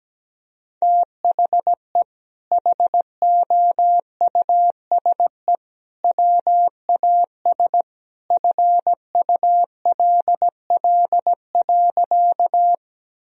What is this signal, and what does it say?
Morse code, heard as a keyed tone, about 17 wpm, 700 Hz. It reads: THE HOUSE WAS FULL.
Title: Sentences from Top 1000 Words - Review Part 3 - 17wpm